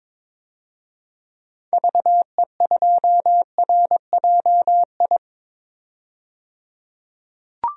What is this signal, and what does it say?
Morse code, heard as a keyed tone, about 22 wpm, 700 Hz. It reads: VE2RJI
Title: DXCC-100 Call Signs - 6 Characters - 22wpm